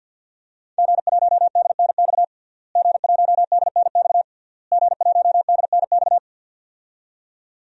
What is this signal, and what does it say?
Morse code, heard as a keyed tone, about 52 wpm, 700 Hz. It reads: G1DNX G1DNX G1DNX
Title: DXCC-100 Call Signs - 5 Characters - Repeated 3x - 50wpm